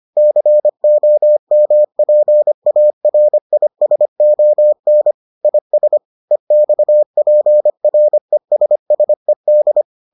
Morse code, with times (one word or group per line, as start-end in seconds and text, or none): 0.2-5.1 COMPARISON
5.5-6.0 IS
6.3-9.8 EXPRESSED